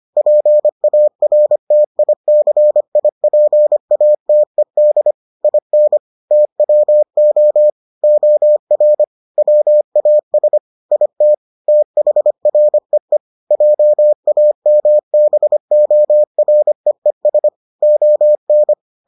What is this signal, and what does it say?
Morse code, heard as a keyed tone, about 25 wpm, 600 Hz. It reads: PARTICIPATED IN TWO OR WAS IT THREE JAMBOREES ON